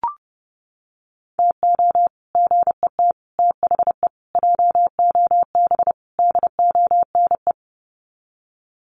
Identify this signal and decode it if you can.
Morse code, heard as a keyed tone, about 30 wpm, 700 Hz. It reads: TO GET THE JOB DONE
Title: Sets of 5 Words Encore2 30wpm